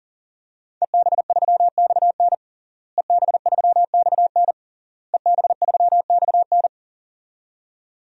Morse code, heard as a keyed tone, about 40 wpm, 700 Hz. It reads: EB3XN EB3XN EB3XN
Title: DXCC-100 Call Signs - 5 Characters - Repeated 3x - 40wpm